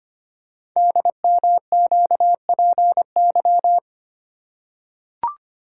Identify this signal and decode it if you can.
Morse code, heard as a keyed tone, about 25 wpm, 700 Hz. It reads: DMQPY
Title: Code Groups - 5 Letters - 25wpm